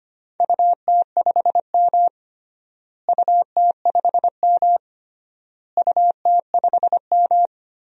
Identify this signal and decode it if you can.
Morse code, heard as a keyed tone, about 25 wpm, 700 Hz. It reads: UT5M UT5M UT5M
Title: DXCC-100 Call Signs - 4 Characters - Repeated 3x - 25wpm